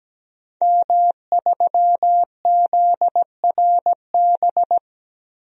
Morse code, heard as a keyed tone, about 17 wpm, 700 Hz. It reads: M3ZRB
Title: DXCC-100 Call Signs - 5 Characters - Repeated 3x - 17wpm